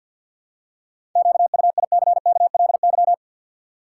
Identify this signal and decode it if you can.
Morse code, heard as a keyed tone, about 53 wpm, 700 Hz. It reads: QUICKLY
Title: Seven Letter Words - 50wpm